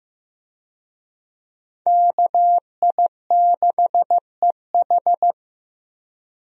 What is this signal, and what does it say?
Morse code, heard as a keyed tone, about 15 wpm, 700 Hz. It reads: KI6EH
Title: USA Call Signs - 5 Characters - Repeated 3x - 15wpm